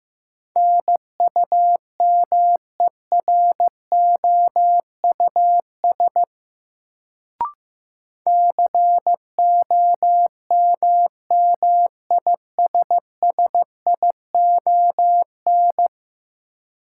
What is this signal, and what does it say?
Morse code, heard as a keyed tone, about 15 wpm, 700 Hz. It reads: NUMEROUS COMMISSION
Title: Sentences from Top 1400 Words - New Word Drill - 15wpm